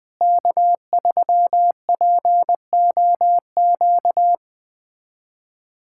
Morse code, heard as a keyed tone, about 20 wpm, 700 Hz. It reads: K3POQ
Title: USA Callsign DE Callsign - 5 Characters - 20wpm